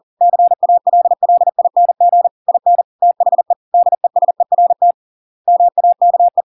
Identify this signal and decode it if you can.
Morse code, heard as a keyed tone, about 40 wpm, 700 Hz. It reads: CALLING IN THE DESERT MAKE